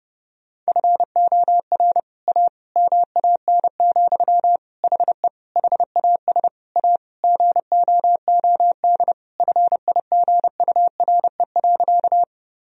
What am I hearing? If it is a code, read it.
Morse code, heard as a keyed tone, about 30 wpm, 700 Hz. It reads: FOR A MAN, HE HAS A GOOD FIGURE.
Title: Sentences from Top 1000 Words - Review Part 2 - 30wpm